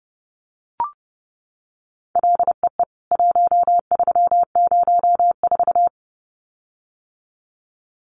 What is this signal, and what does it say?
Morse code, heard as a keyed tone, about 28 wpm, 700 Hz. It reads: LEE 1304
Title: CWT Exchange 30wpm